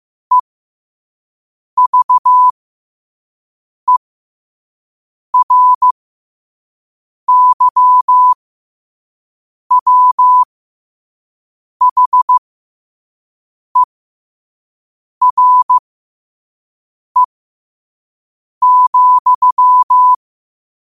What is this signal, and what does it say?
Morse code, heard as a keyed tone, about 15 wpm, spaced as for 5 wpm, 1000 Hz. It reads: EVERYWHERE,
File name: qotdfiles / 5wpm